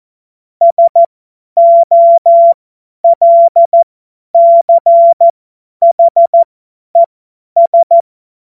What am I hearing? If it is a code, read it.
Morse code, heard as a keyed tone, about 14 wpm, spaced as for 10 wpm, 680 Hz. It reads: SOLCHES
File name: Morsekurs / Witze / Witz 16 Januar 2024 50 F70